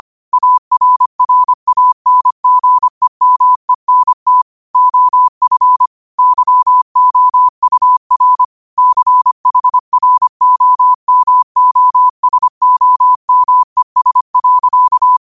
Morse code, heard as a keyed tone, about 25 wpm, 1000 Hz. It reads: ARRANGEMENT OF YOUR CHROMOSOMES.
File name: qotdfiles / 25wpm